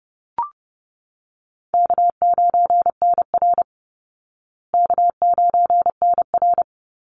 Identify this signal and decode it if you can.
Morse code, heard as a keyed tone, about 30 wpm, 700 Hz. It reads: K9NR K9NR
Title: Easy Call Signs Repeated 2x - 30wpm